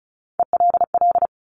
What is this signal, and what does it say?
Morse code, heard as a keyed tone, about 36 wpm, 700 Hz. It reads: ELL
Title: Top Three Letter Combinations in Words 35wpm